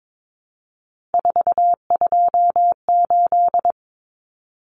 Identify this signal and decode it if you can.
Morse code, heard as a keyed tone, about 22 wpm, 700 Hz. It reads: 428
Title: Common Cut Numbers - 3-Digit 22wpm